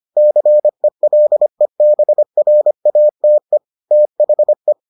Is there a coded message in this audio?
Yes: Morse code from 0.2 to 4.7 s, heard as a keyed tone, about 25 wpm, 600 Hz. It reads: CELEBRATE THE